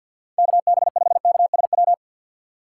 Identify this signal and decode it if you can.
Morse code, heard as a keyed tone, about 51 wpm, 700 Hz. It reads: KB5KSW
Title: USA Call Signs - 6 Characters - 8x Character Spacing - 50wpm